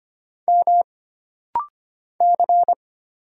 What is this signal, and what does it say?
Morse code, heard as a keyed tone, about 25 wpm, 700 Hz. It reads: M C
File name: Single Character (Letter:Number:Punctuation:Prosign) - Rapid-Fire - 25wpm